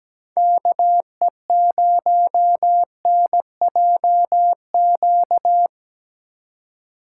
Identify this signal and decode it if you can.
Morse code, heard as a keyed tone, about 20 wpm, 700 Hz. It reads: KE0NJQ
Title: Speed Racing - USA 2x3 Call Signs 17wpm